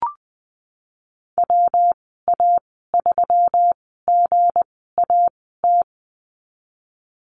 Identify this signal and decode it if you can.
Morse code, heard as a keyed tone, about 20 wpm, spaced as for 15 wpm, 700 Hz. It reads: WA3GAT
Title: 193 Lesson 31g - Letter v - Call Signs 20wpm